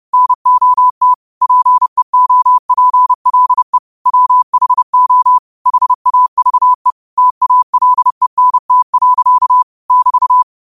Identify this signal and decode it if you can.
Morse code, heard as a keyed tone, about 30 wpm, 1000 Hz. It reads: NOT PEOPLE WHO HAVE TALENT. <BT>
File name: qotdfiles / 30wpm